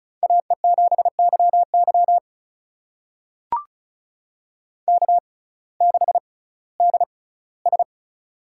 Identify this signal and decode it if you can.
Morse code, heard as a keyed tone, about 34 wpm, spaced as for 19 wpm, 700 Hz. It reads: AE7YY K6DS
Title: USA Call Signs - 6x Character Spacing - 35wpm